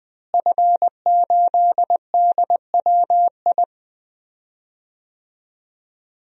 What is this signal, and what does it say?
Morse code, heard as a keyed tone, about 20 wpm, 700 Hz. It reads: F8DWI
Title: DXCC-100 Call Signs - 5 Characters - Repeated 3x - 20wpm